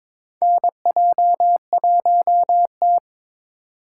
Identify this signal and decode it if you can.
Morse code, heard as a keyed tone, about 21 wpm, 700 Hz. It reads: NJ1T
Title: K1USN SST QSO Practice - 22wpm